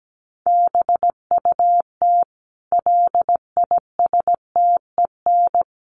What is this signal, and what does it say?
Morse code, heard as a keyed tone, about 17 wpm, 700 Hz. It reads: BUT LISTEN
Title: Sentences from Top 400 Words 17wpm